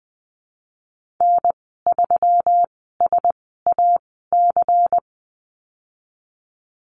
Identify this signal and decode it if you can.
Morse code, heard as a keyed tone, about 20 wpm, spaced as for 15 wpm, 700 Hz. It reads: N3SAC
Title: 128 Lesson 21g - Number 6 - Call Signs 20wpm